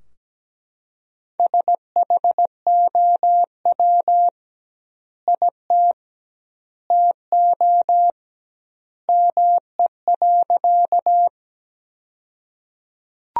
Morse code, heard as a keyed tone, about 17 wpm, 700 Hz. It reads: SHOW IT TO ME.
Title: 2x Word-Spacing - Sentences from Top 200 Words - 17wpm